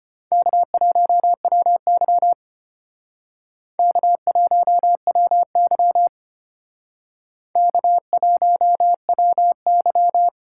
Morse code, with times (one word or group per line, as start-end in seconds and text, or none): 0.3-2.3 K1WY
3.8-6.1 K1WY
7.6-10.3 K1WY